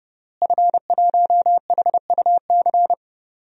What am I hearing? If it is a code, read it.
Morse code, heard as a keyed tone, about 31 wpm, 700 Hz. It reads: F1HUC
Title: DXCC-100 Call Signs - 5 Characters - Repeated 3x - 30wpm